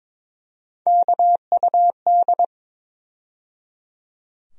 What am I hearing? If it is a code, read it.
Morse code, heard as a keyed tone, about 22 wpm, 700 Hz. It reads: KUD
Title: USA Call Sign Postfixes - 3 Letters 22wpm